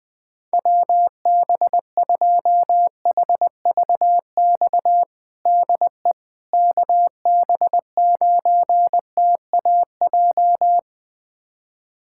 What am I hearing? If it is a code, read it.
Morse code, heard as a keyed tone, about 20 wpm, 700 Hz. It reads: WB2HVX DE KB9TAJ
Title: USA Callsign DE Callsign - 6 Characters - 20wpm